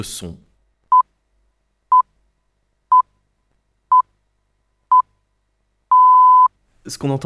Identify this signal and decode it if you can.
Radio time signal (time pips), with speech. It is heard from 0.9 to 6.5 s.